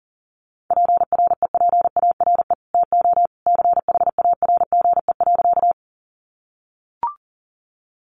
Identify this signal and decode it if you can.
Morse code, heard as a keyed tone, about 40 wpm, 700 Hz. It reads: PREPARE TO CHARGE.
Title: Sentences from Top 900 Words 40wpm